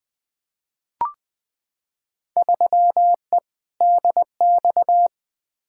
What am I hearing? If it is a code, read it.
Morse code, heard as a keyed tone, about 20 wpm, 700 Hz. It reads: 3E DX
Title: Field Day Exchanges Encore - 20wpm